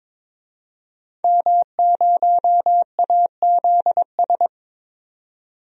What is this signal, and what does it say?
Morse code, heard as a keyed tone, about 22 wpm, 700 Hz. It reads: M0AZS
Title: DXCC-100 Call Signs - 5 Characters - 22wpm